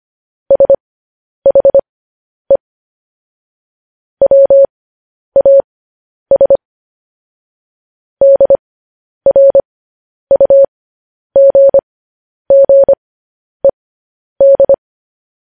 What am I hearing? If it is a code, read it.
Morse code, heard as a keyed tone, about 25 wpm, spaced as for 10 wpm, 550 Hz. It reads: SHE WAS DRUGGED